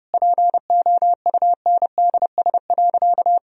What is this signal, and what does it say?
Morse code, heard as a keyed tone, about 32 wpm, 700 Hz. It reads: POUNDS.
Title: Sentences from Top 1000 Words - Review Part 3 - 30wpm